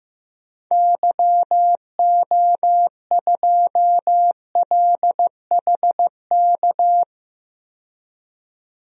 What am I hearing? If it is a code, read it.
Morse code, heard as a keyed tone, about 15 wpm, 700 Hz. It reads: YO2LHK
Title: DXCC-100 Call Signs - 6 Characters - 15wpm